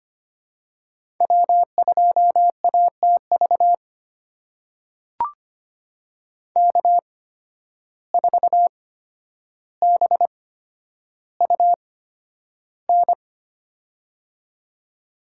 Morse code, heard as a keyed tone, about 25 wpm, spaced as for 11 wpm, 700 Hz. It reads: W2ATV K4BUN
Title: USA Call Signs - 5 Characters - 8x Character Spacing - 25wpm